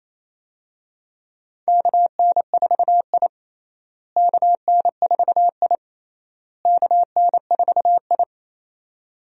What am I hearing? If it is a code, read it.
Morse code, heard as a keyed tone, about 28 wpm, 700 Hz. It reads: KN4I KN4I KN4I
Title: USA Call Signs - 4 Characters - Repeated 3x - 28wpm